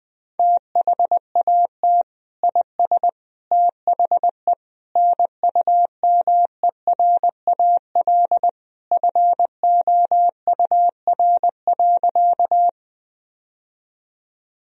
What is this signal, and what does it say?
Morse code, heard as a keyed tone, about 21 wpm, 700 Hz. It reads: THAT IS THE NUMERAL FOUR.
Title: Sentences from Top 1000 Words - Review Part 2 - 20wpm